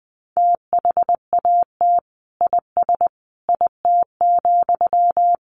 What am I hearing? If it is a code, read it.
Morse code, heard as a keyed tone, about 19 wpm, 700 Hz. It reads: THAT IS IT,
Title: Sentences from Top 800 Words 20wpm